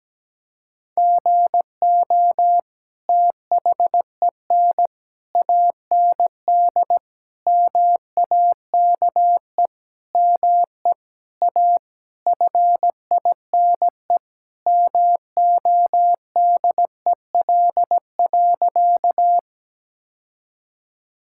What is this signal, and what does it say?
Morse code, heard as a keyed tone, about 17 wpm, 700 Hz. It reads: GO THEN AND MAKE ME A FINE MODEL.
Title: Sentences from Top 1000 Words - Review Part 5 - 17wpm